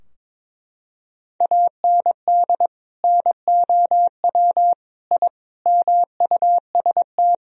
Morse code, heard as a keyed tone, about 22 wpm, 700 Hz. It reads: AND NOW I MUST
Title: Sentences from Top 1000 Words - Review Part 3 - 22wpm